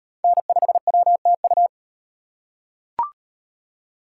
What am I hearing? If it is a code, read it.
Morse code, heard as a keyed tone, about 35 wpm, 700 Hz. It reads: N5WTU